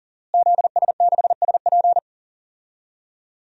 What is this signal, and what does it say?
Morse code, heard as a keyed tone, about 40 wpm, 700 Hz. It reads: ZS6SP
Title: DXCC-100 Call Signs - 5 Characters - Repeated 3x - 40wpm